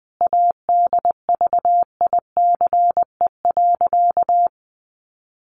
Morse code, heard as a keyed tone, about 20 wpm, 700 Hz. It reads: ADVICE.
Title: Sentences from Top 1100 Words 20wpm